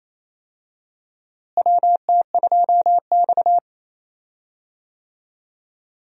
Morse code, heard as a keyed tone, about 27 wpm, 700 Hz. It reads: WT2X
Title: USA Call Signs - 4 Characters - Repeated 2x - 28wpm